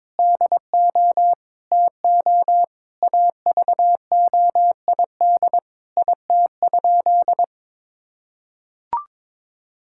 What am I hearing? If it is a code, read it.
Morse code, heard as a keyed tone, about 21 wpm, 700 Hz. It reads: DO TO AVOID IT?
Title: Sentences from Top 1000 Words - Review Part 6 - 22wpm